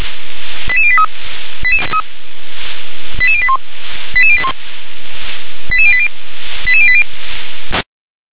Five-tone selective call (ZVEI), heard as quick runs of five tones, tones of about 70 ms. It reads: 88083 88081 88081 88080 88080